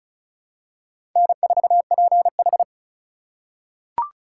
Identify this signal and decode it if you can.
Morse code, heard as a keyed tone, about 36 wpm, 700 Hz. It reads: N4PH
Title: USA Call Signs - 4 Characters - 4x Character Spacing - 35wpm